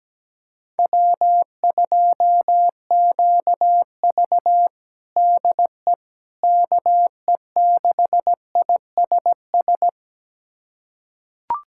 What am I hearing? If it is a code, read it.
Morse code, heard as a keyed tone, about 17 wpm, 700 Hz. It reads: W2QV DE KE6ISS
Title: USA Callsign DE Callsign 17wpm